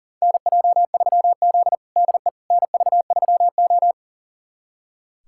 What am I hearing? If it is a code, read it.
Morse code, heard as a keyed tone, about 40 wpm, 700 Hz. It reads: NJ3Z DE NV3O
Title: USA Callsign DE Callsign - 4 Characters - 40wpm